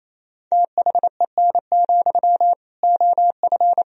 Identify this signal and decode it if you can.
Morse code, heard as a keyed tone, about 28 wpm, 700 Hz. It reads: THEN, OF
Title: Sentences from Top 1000 Words - Review Part 1 - 28wpm